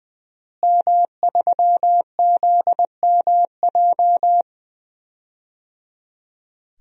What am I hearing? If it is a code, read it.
Morse code, heard as a keyed tone, about 20 wpm, 700 Hz. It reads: M3ZMJ
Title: DXCC-100 Call Signs - 5 Characters - Repeated 3x - 20wpm